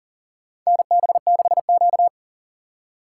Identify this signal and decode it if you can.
Morse code, heard as a keyed tone, about 40 wpm, 700 Hz. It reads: NB6Q
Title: USA Call Signs - 2x Character Spacing - 40wpm